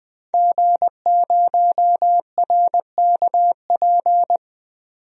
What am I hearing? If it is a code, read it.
Morse code, heard as a keyed tone, about 20 wpm, 700 Hz. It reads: G0RKP
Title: DXCC-100 Call Signs - 5 Characters - 20wpm